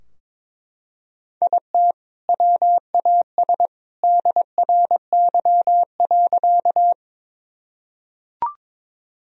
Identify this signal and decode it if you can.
Morse code, heard as a keyed tone, about 22 wpm, 690 Hz. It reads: IT WAS DRY.
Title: Sentences from Top 1000 Words - Review Part 3 - 22wpm